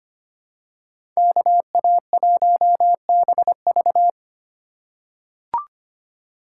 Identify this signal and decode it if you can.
Morse code, heard as a keyed tone, about 25 wpm, 700 Hz. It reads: KA1BV
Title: USA Call Signs - 5 Characters - 3x Character Spacing - 25wpm